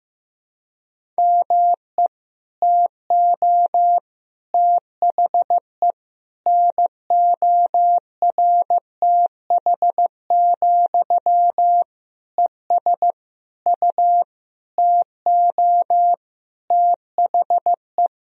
Morse code, heard as a keyed tone, about 15 wpm, 700 Hz. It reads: ME TO THE NORTH, ES U TO THE
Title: Sentences from Top 400 Words and Abbreviations 15wpm